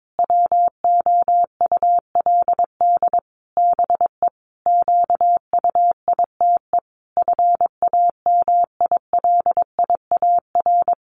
Morse code, heard as a keyed tone, about 22 wpm, 700 Hz. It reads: WOULD BE QUITE FAMILIAR
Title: Sentences from Top 1000 Words 22wpm